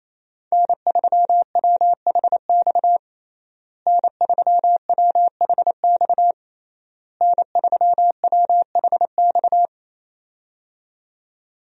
Morse code, heard as a keyed tone, about 27 wpm, 700 Hz. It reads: N3WHX N3WHX N3WHX
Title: USA Call Signs - 5 Characters - Repeated 3x - 28wpm